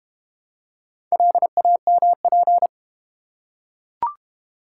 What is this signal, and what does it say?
Morse code, heard as a keyed tone, about 31 wpm, 700 Hz. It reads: LAMP
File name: Easy Words - Speed-Racing with Hints - 25wpm